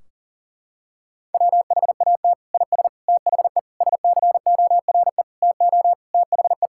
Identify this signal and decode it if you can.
Morse code, heard as a keyed tone, about 37 wpm, 700 Hz. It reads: WHAT IS THE SCORE TO THE
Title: Sentences from Top 1600 Words 40wpm